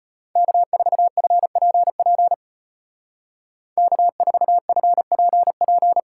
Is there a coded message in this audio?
Yes: Morse code from 0.4 to 6.0 s, heard as a keyed tone, about 32 wpm, 700 Hz. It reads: K4FPP K4FPP